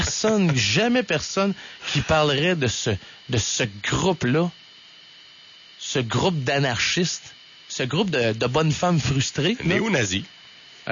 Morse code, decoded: none